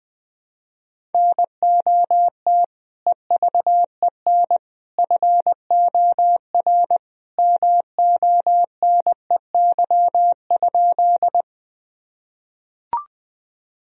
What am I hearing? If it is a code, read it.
Morse code, heard as a keyed tone, about 21 wpm, 700 Hz. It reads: NOT EVEN FOR MONEY?
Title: Sentences from Top 1000 Words - Review Part 1 - 20wpm